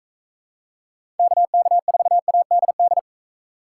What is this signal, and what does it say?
Morse code, heard as a keyed tone, about 41 wpm, 700 Hz. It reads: KK4ADD